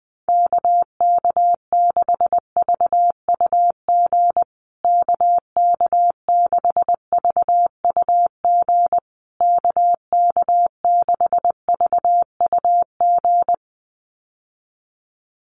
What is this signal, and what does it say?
Morse code, heard as a keyed tone, about 20 wpm, 700 Hz. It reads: KK6VUG KK6VUG KK6VUG
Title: USA Call Signs Repeated 3x 20wpm